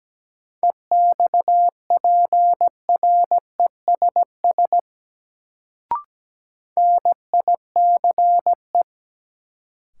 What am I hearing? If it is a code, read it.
Morse code, heard as a keyed tone, about 17 wpm, 700 Hz. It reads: EXPRESS NICE
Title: Sentences from Top 1100 Words - New Word Drill - 17wpm